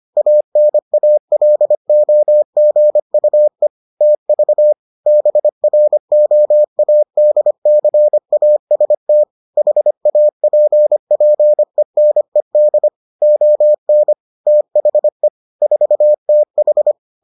Morse code, heard as a keyed tone, about 25 wpm, 600 Hz. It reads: ANALOGUE TV BROADCAST HAPPENED ON THE 4TH